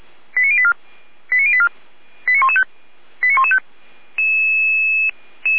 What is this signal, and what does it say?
Five-tone selective call (ZVEI), heard as quick runs of five tones, tones of about 70 ms. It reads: 89084 89084 89115 89115